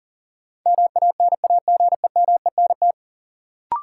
Morse code, heard as a keyed tone, about 40 wpm, 700 Hz. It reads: MANAGEMENT
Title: Sentences from Top 1000 Words - Review Part 4 - New Word Drill - 40wpm